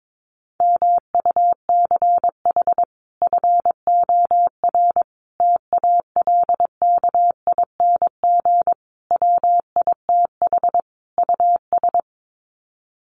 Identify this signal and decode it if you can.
Morse code, heard as a keyed tone, about 22 wpm, 700 Hz. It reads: MUCH FOR TALKING WITH US
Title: Sets of 5 Words Encore 22wpm